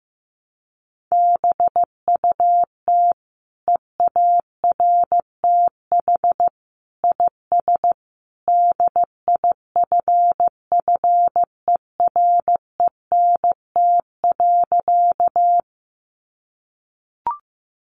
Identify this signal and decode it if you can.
Morse code, heard as a keyed tone, about 15 wpm, 700 Hz. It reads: BUT EARTH IS DIFFERENT.
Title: Sentences from Top 600 Words 15wpm